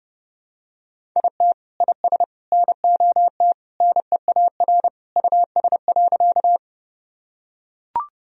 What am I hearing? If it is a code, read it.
Morse code, heard as a keyed tone, about 30 wpm, 700 Hz. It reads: IT IS NOT NEAR US.
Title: Sentences from Top 1000 Words - Review Part 1 - 30wpm